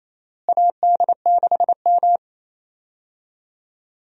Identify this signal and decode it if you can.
Morse code, heard as a keyed tone, about 28 wpm, 700 Hz. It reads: AD6M
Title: USA Call Signs - 4 Characters - Repeated 2x - 28wpm